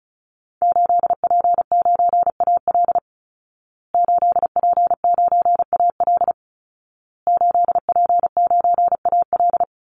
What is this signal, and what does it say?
Morse code, heard as a keyed tone, about 34 wpm, 700 Hz. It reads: 8P9AL 8P9AL 8P9AL
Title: Call Signs Repeated 2x - 35wpm